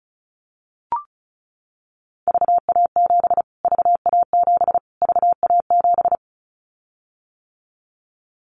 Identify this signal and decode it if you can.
Morse code, heard as a keyed tone, about 34 wpm, 700 Hz. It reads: VA7 VA7 VA7
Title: Callsign Prefixes for Top 25 DXCC - Repeated 3x - 35wpm